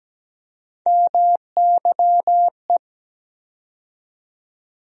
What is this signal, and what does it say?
Morse code, heard as a keyed tone, about 17 wpm, 700 Hz. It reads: MYE
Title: USA Call Sign Postfixes - 3 Letters 17wpm